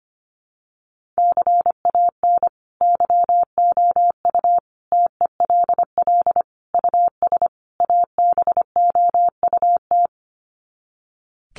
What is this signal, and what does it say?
Morse code, heard as a keyed tone, about 25 wpm, 700 Hz. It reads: CAN YOU TELL US ABOUT
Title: Sets of 5 Words Encore2 25wpm